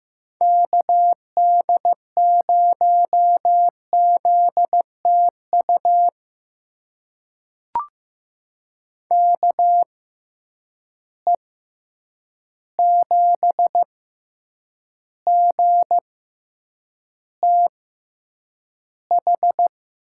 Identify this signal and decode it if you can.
Morse code, heard as a keyed tone, about 15 wpm, spaced as for 8 wpm, 700 Hz. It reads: KD0ZTU KE7GTH
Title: USA Call Signs - 6 Characters - 6x Character Spacing - 15wpm